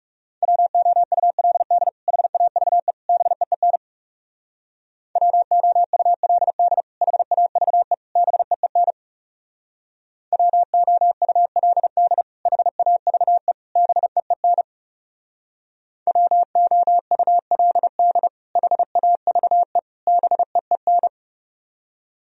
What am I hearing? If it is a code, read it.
Morse code, heard as a keyed tone, about 39 wpm, 700 Hz. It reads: WOULD HAVE BEEN WOULD HAVE BEEN WOULD HAVE BEEN WOULD HAVE BEEN